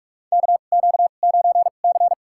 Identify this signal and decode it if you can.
Morse code, heard as a keyed tone, about 45 wpm, spaced as for 32 wpm, 700 Hz. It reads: KQ9C
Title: USA Call Signs - 4 Characters - 2x Character Spacing - 45wpm